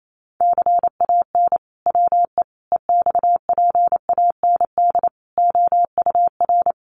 Morse code, heard as a keyed tone, about 28 wpm, 700 Hz. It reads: CAN WE EXPAND OUR
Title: Sentences from Top 1400 Words 28wpm